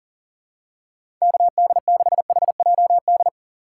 Morse code, heard as a keyed tone, about 40 wpm, 700 Hz. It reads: KD6HJD
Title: USA Call Signs - 6 Characters - 4x Character Spacing - 40wpm